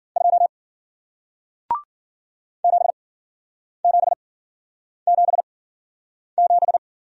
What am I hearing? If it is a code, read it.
Morse code, heard as a keyed tone, about 53 wpm, spaced as for 9 wpm, 700 Hz. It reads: 2 7777